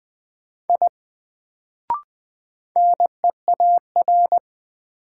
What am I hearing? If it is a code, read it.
Morse code, heard as a keyed tone, about 20 wpm, 700 Hz. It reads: I NEAR
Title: Sentences from Top 200 Words - New Word Drill - 20wpm